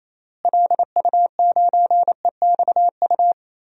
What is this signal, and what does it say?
Morse code, heard as a keyed tone, about 27 wpm, 700 Hz. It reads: LU9EXU
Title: DXCC-100 Call Signs - 6 Characters - Repeated 3x - 28wpm